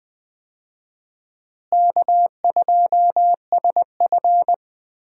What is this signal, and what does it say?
Morse code, heard as a keyed tone, about 20 wpm, 700 Hz. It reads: K2SF
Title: USA Call Signs - 4 Characters - Repeated 3x - 20wpm